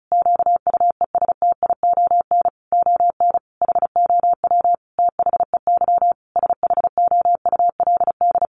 Morse code, heard as a keyed tone, about 34 wpm, 700 Hz. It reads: QUESTION ON HOW THEY SHOULD
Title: Sentences from Top 1300 Words 35wpm